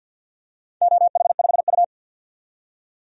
Morse code, heard as a keyed tone, about 51 wpm, 700 Hz. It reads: OH5U
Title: DXCC-100 Call Signs - 4 Characters - 50wpm